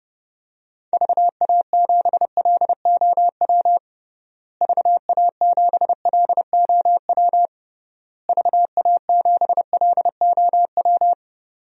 Morse code, heard as a keyed tone, about 32 wpm, 700 Hz. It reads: VA7LOW VA7LOW VA7LOW
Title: DXCC-100 Call Signs - 6 Characters - Repeated 3x - 30wpm